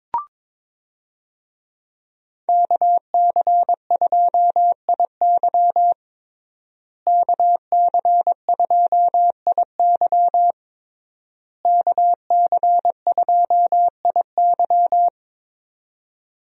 Morse code, heard as a keyed tone, about 22 wpm, 700 Hz. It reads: KC2IY KC2IY KC2IY
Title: USA Call Signs - 5 Characters - Repeated 3x - 22wpm